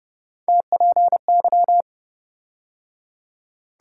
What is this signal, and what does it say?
Morse code, heard as a keyed tone, about 29 wpm, 700 Hz. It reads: TPY